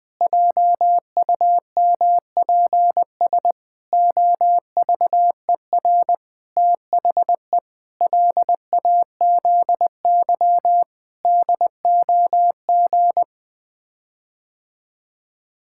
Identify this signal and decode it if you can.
Morse code, heard as a keyed tone, about 20 wpm, 700 Hz. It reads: JUMPS OVER THE LAZY DOG
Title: Sets of 8 Words - 20wpm